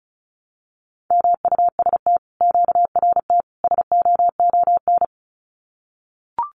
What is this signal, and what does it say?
Morse code, heard as a keyed tone, about 36 wpm, 700 Hz. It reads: MUST QRT SOON
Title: Common QSO Phrases Encore 35wpm